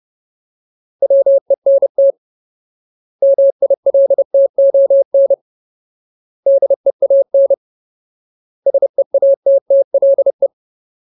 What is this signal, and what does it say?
Morse code, heard as a keyed tone, about 30 wpm, 550 Hz. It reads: WENT MILTON DEAN SEATTLE
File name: Ultimate Word List / 30WPM Ultimate